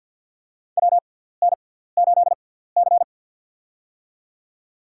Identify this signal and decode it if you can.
Morse code, heard as a keyed tone, about 53 wpm, spaced as for 17 wpm, 700 Hz. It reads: WN8C
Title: USA Call Signs - 6x Character Spacing - 50wpm